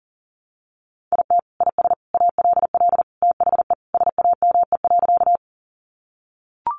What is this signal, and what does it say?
Morse code, heard as a keyed tone, about 40 wpm, 700 Hz. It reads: IT IS ALL THE SAME.
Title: Sentences from Top 200 Words 40wpm